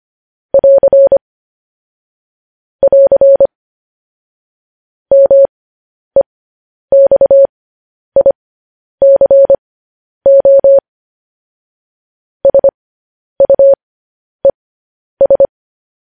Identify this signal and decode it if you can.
Morse code, heard as a keyed tone, about 25 wpm, spaced as for 10 wpm, 550 Hz. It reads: <AR> <AR> MEXICO SUES
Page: news-headlines-in-morse-code-at-10-wpm